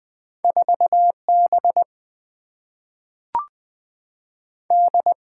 Morse code, heard as a keyed tone, about 20 wpm, 700 Hz. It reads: 4B D
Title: Single-Letter Single-Number 20wpm